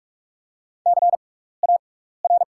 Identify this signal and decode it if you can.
Morse code, heard as a keyed tone, about 44 wpm, spaced as for 15 wpm, 690 Hz. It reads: CAR
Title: 6x Character Spacing - Top 500 Words - 45wpm